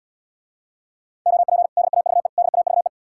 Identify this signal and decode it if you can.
Morse code, heard as a keyed tone, about 80 wpm, 700 Hz. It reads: OP DAVE DAVE